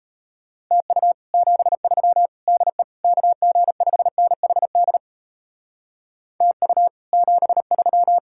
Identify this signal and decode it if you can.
Morse code, heard as a keyed tone, about 36 wpm, 700 Hz. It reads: TU 73 DE KG5NHD TU 73